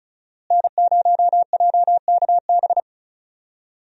Morse code, heard as a keyed tone, about 34 wpm, 700 Hz. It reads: N0JKB
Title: USA Call Signs - 5 Characters - Repeated 2x - 35wpm